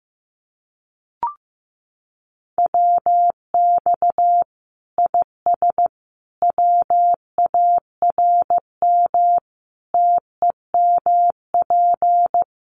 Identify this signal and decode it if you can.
Morse code, heard as a keyed tone, about 15 wpm, 700 Hz. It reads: WX IS WARM TEMP
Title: Common QSO Phrases 15wpm